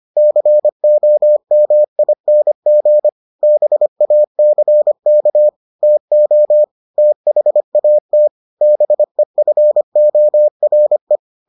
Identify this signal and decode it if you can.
Morse code, heard as a keyed tone, about 25 wpm, 600 Hz. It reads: COMING BACK TO THAT BEFORE